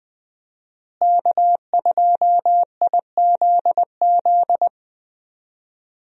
Morse code, heard as a keyed tone, about 20 wpm, 700 Hz. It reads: K2IZZ
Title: USA Call Signs - 5 Characters - Repeated 2x - 20wpm